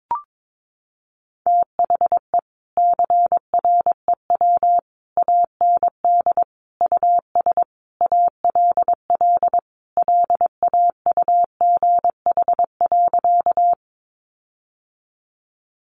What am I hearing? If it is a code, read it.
Morse code, heard as a keyed tone, about 22 wpm, 700 Hz. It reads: THE CREW AND US ALL LAUGH.
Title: Sentences from Top 1500 Words 22wpm